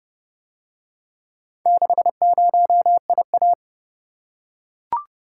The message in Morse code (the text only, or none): 60IA